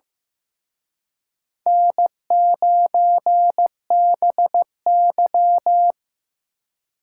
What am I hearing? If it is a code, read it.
Morse code, heard as a keyed tone, about 18 wpm, 700 Hz. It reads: N9BY
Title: Speed Racing - USA 1x2 Call Signs 15wpm